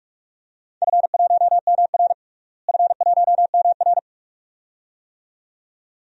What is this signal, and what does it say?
Morse code, heard as a keyed tone, about 43 wpm, 700 Hz. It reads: F1MR F1MR
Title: DXCC-100 Call Signs - 4 Characters - Repeated 2x - 45wpm